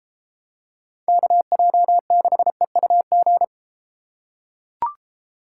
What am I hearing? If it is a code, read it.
Morse code, heard as a keyed tone, about 32 wpm, 700 Hz. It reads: KJ6EUG